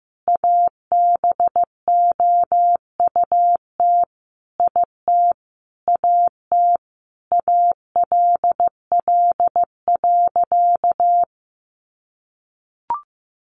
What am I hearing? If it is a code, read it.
Morse code, heard as a keyed tone, about 15 wpm, 700 Hz. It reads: ABOUT IT AT ALL.
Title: Sentences from Top 800 Words 15wpm